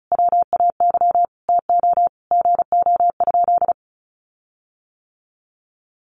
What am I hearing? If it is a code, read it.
Morse code, heard as a keyed tone, about 34 wpm, 700 Hz. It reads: WAY TO GO?
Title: Sentences from Top 1300 Words 35wpm